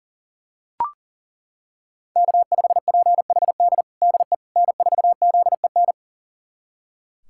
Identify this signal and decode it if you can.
Morse code, heard as a keyed tone, about 40 wpm, 700 Hz. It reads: K5PHD DE N4ZEN
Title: USA Callsign DE Callsign - 5 Characters - 40wpm